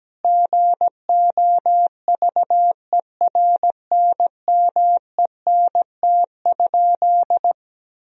Morse code, heard as a keyed tone, about 17 wpm, 700 Hz. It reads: GOVERNMENT?
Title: Sentences from Top 1000 Words - Review Part 3 - 17wpm